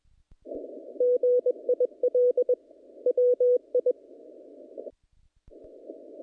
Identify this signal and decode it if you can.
Morse code, heard as a keyed tone, about 22 wpm, 500 Hz. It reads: GIL WI
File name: contests / naqp-cw-k5tr-op-k5pi- -2000